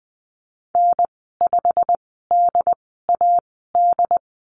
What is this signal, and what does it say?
Morse code, heard as a keyed tone, about 20 wpm, spaced as for 14 wpm, 700 Hz. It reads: N5DAD
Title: 161 Lesson 26g - Letter g - Call Signs 20wpm